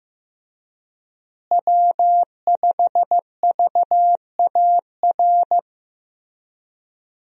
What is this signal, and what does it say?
Morse code, heard as a keyed tone, about 15 wpm, 700 Hz. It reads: W5VAR
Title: USA Call Signs - 5 Characters - Repeated 3x - 15wpm